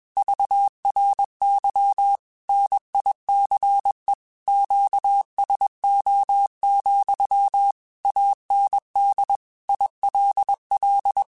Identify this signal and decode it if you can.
Morse code, heard as a keyed tone, about 21 wpm, 800 Hz. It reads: VRY NICE QSO, AND ILL